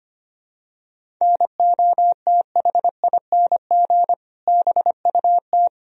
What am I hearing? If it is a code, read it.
Morse code, heard as a keyed tone, about 25 wpm, 700 Hz. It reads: NOTHING BUT
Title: Sentences from Top 1000 Words - Review Part 1 - 25wpm